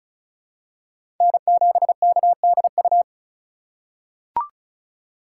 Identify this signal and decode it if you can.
Morse code, heard as a keyed tone, about 34 wpm, 700 Hz. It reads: N7KDU
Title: USA Call Signs - 5 Characters - 2x Character Spacing - 35wpm